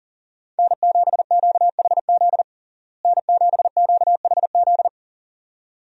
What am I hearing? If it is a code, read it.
Morse code, heard as a keyed tone, about 40 wpm, 700 Hz. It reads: N7QHZ N7QHZ
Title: USA Call Signs - 5 Characters - Repeated 2x - 40wpm